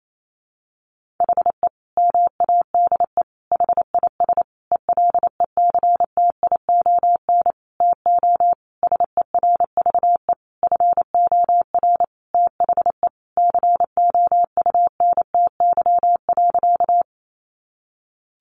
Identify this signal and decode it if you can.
Morse code, heard as a keyed tone, about 28 wpm, 700 Hz. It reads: HE MADE HIS ELECTION TO SERVE FOR THE COUNTY.
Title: Sentences from Top 1500 Words 28wpm